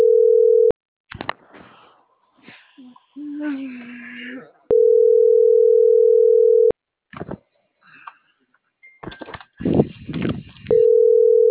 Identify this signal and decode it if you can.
Ringback tone (North America), 440 + 480 Hz.